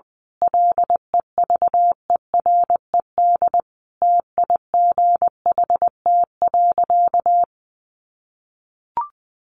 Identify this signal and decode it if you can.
Morse code, heard as a keyed tone, about 20 wpm, 700 Hz. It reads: LEVERED TIGHT.
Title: A Sound Of Thunder 20wpm